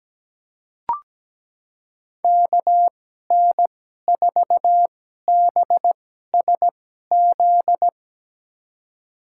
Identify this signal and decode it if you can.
Morse code, heard as a keyed tone, about 17 wpm, spaced as for 12 wpm, 700 Hz. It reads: KN4BSZ
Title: USA Call Signs - 6 Characters - 2x Character Spacing - 17wpm